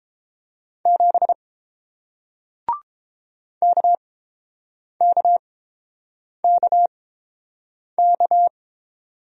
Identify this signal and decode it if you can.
Morse code, heard as a keyed tone, about 30 wpm, spaced as for 7 wpm, 700 Hz. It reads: 7 KKKK